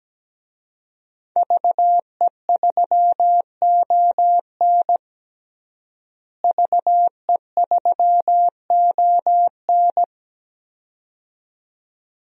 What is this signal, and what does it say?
Morse code, heard as a keyed tone, about 17 wpm, 700 Hz. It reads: VE3ON VE3ON
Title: DXCC-100 Call Signs - 5 Characters - Repeated 2x - 17wpm